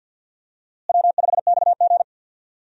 Morse code, heard as a keyed tone, about 47 wpm, 700 Hz. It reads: W5XG
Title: USA Call Signs - 4 Characters - Repeated 2x - 50wpm